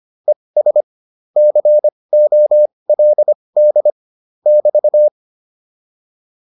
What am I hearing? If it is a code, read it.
Morse code, heard as a keyed tone, about 25 wpm, spaced as for 20 wpm, 600 Hz. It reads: ES COLD <BT>
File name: QSO310a 20